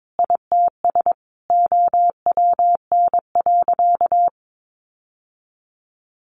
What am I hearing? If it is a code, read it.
Morse code, heard as a keyed tone, about 22 wpm, 700 Hz. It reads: ITS OWN.
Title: Sentences from Top 1200 Words 22wpm